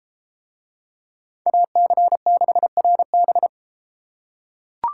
Morse code, heard as a keyed tone, about 33 wpm, 700 Hz. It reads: AC6RB